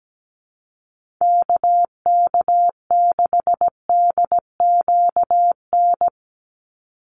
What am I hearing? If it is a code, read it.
Morse code, heard as a keyed tone, about 17 wpm, 700 Hz. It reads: KK6DQN
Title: USA Call Signs 17wpm